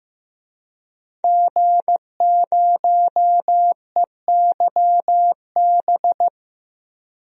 Morse code, heard as a keyed tone, about 15 wpm, 700 Hz. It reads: G0EYB
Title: DXCC-100 Call Signs - 5 Characters - 15wpm